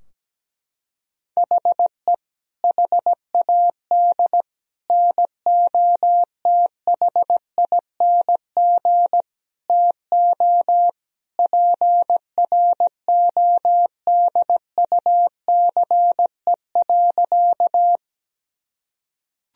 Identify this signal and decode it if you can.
Morse code, heard as a keyed tone, about 17 wpm, 700 Hz. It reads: HE HAD NOTHING TO PRODUCE.
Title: Sentences from Top 1000 Words - Review Part 3 - 17wpm